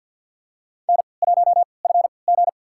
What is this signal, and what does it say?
Morse code, heard as a keyed tone, about 53 wpm, spaced as for 29 wpm, 700 Hz. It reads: N1FG
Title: USA Call Signs - 4 Characters - 3x Character Spacing - 50wpm